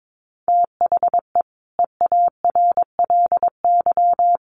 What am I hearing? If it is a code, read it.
Morse code, heard as a keyed tone, about 22 wpm, 700 Hz. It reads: THE EARLY
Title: Sets of 5 Words Encore 22wpm